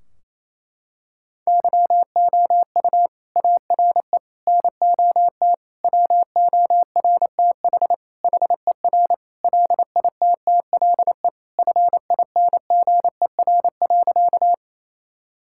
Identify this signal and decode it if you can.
Morse code, heard as a keyed tone, about 28 wpm, 700 Hz. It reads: YOU ARE NOT WORTH HER LITTLE FINGER.
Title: Sentences from Top 1600 Words 28wpm